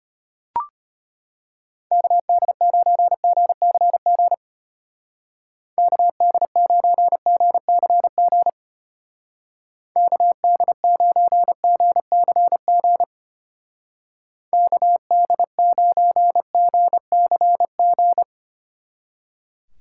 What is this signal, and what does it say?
Morse code, heard as a keyed tone, about 32 wpm, 700 Hz. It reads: KD9GCG KD9GCG KD9GCG KD9GCG